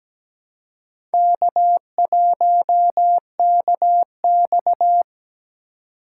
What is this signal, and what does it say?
Morse code, heard as a keyed tone, about 17 wpm, 700 Hz. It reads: K1KX
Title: USA Call Signs - 4 Characters - 3x Character Spacing - 17wpm